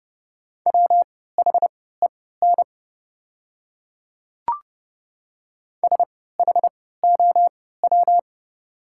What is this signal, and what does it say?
Morse code, heard as a keyed tone, about 30 wpm, spaced as for 17 wpm, 690 Hz. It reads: WHEN SHOW
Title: 3x Character Spacing - Top 500 Words - 30wpm